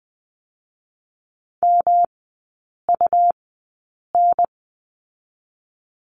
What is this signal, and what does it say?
Morse code, heard as a keyed tone, about 20 wpm, spaced as for 8 wpm, 700 Hz. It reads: MUN
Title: 145 Lesson 24e - Letter y - Three Characters 20wpm